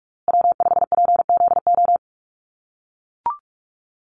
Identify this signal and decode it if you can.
Morse code, heard as a keyed tone, about 44 wpm, 700 Hz. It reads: W5PZO
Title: USA Call Signs Repeated 2x 45wpm